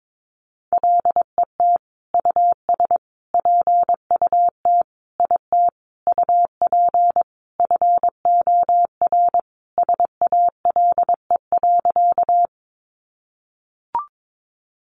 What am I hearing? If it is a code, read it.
Morse code, heard as a keyed tone, about 22 wpm, 700 Hz. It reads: LET US PUT IT UP FOR SALE.
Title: Sentences from Top 1400 Words 22wpm